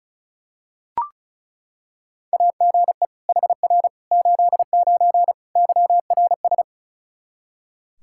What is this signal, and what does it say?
Morse code, heard as a keyed tone, about 34 wpm, 700 Hz. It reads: AGE HR 89 YRS
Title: Practice QSO - Exchange 3 - More Info - 35wpm